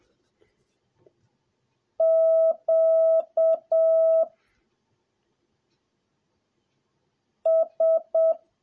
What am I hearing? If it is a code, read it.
Morse code, heard as a keyed tone, about 7 wpm, 650 Hz. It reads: Q S